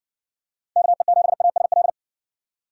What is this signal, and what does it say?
Morse code, heard as a keyed tone, about 56 wpm, 700 Hz. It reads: KE7ASL